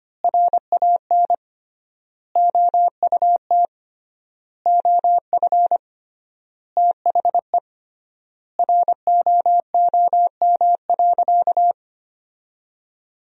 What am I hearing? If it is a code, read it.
Morse code, heard as a keyed tone, about 25 wpm, 700 Hz. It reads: RAN OUT OF THE ROOM.
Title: 3x Word-Spacing - Sentences from Top 500 Words - 25wpm